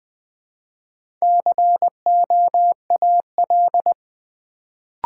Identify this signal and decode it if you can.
Morse code, heard as a keyed tone, about 20 wpm, 700 Hz. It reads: COAL
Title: Build A Word - 1 Letter At A Time - 3 to 5-Letter Words - 20wpm